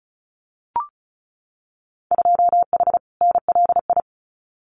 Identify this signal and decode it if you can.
Morse code, heard as a keyed tone, about 36 wpm, 700 Hz. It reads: 2H NLI
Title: Winter Field Day Exchanges Encore 35wpm